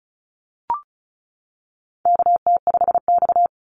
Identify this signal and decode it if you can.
Morse code, heard as a keyed tone, about 34 wpm, 700 Hz. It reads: KT5X
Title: NA SOTA Activator Callsigns 35wpm